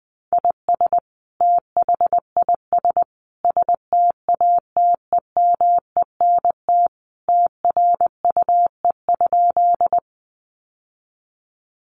Morse code, heard as a keyed tone, about 20 wpm, 700 Hz. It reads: IS THIS STATEMENT TRUE?
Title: Sentences from Top 1200 Words 20wpm